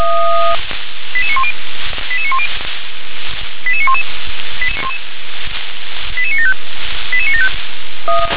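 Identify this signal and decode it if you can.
Five-tone selective call (ZVEI), heard as quick runs of five tones, tones of about 70 ms. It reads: 88910 88910 88911 88975 88975
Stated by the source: ZVEI five-tone selective call received via SDR.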